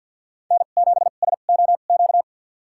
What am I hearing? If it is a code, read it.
Morse code, heard as a keyed tone, about 52 wpm, spaced as for 37 wpm, 700 Hz. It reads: N7SOQ
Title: USA Call Signs - 5 Characters - 2x Character Spacing - 50wpm